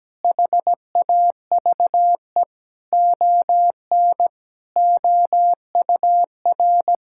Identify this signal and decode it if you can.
Morse code, heard as a keyed tone, about 17 wpm, 700 Hz. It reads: HAVE ON OUR
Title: Sentences from Top 1000 Words - Review Part 6 - 17wpm